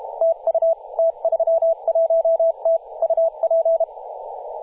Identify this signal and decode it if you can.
Morse code, heard as a keyed tone, about 32 wpm, 660 Hz, with radio noise. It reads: TU T31T UP
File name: posts / 2016-oct-05-2124z-t31t-18-072mhz-cw